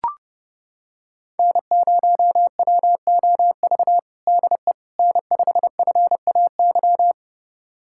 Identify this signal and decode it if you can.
Morse code, heard as a keyed tone, about 32 wpm, 700 Hz. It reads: N0WOV DE N5FAY
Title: USA Callsign DE Callsign - 5 Characters - 30wpm